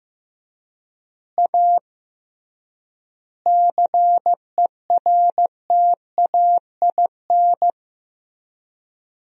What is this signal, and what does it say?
Morse code, heard as a keyed tone, about 15 wpm, 700 Hz. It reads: A CERTAIN
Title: 3x Word-Spacing - Sentences from Top 500 Words - 15wpm